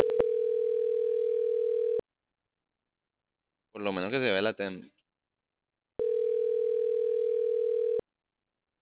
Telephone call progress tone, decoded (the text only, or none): ringback tone